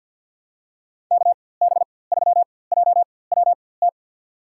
Morse code, heard as a keyed tone, about 52 wpm, spaced as for 23 wpm, 690 Hz. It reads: KB3JWT
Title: USA Call Signs - 4x Character Spacing - 50wpm